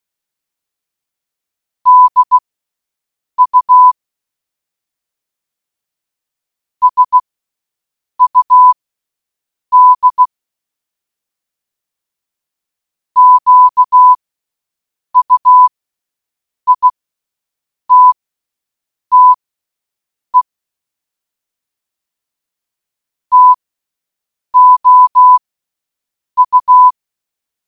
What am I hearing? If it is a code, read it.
Morse code, heard as a keyed tone, about 16 wpm, spaced as for 7 wpm, 1000 Hz. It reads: DU SUD QUITTE TOU